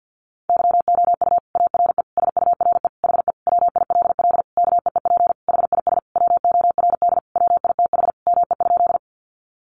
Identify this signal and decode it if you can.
Morse code, heard as a keyed tone, about 50 wpm, 700 Hz. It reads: YOU ARE SURE HE WILL KEEP HIS WORD WITH ME?
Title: Sentences from Top 300 Words 50wpm